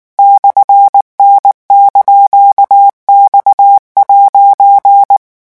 Morse code, heard as a keyed tone, about 19 wpm, 800 Hz. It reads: /N)X'
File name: morse mp3 / revision17-vitesse 18 mots